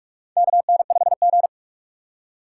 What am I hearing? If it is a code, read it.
Morse code, heard as a keyed tone, about 46 wpm, 700 Hz. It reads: KN5G
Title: USA Call Signs - 4 Characters - 3x Character Spacing - 45wpm